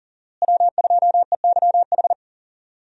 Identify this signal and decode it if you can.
Morse code, heard as a keyed tone, about 40 wpm, 700 Hz. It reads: W2EYH
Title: USA Call Signs - 5 Characters - Repeated 2x - 40wpm